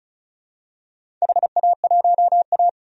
Morse code, heard as a keyed tone, about 36 wpm, 690 Hz. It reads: HA1A
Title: DXCC-100 Call Signs - 4 Characters - 35wpm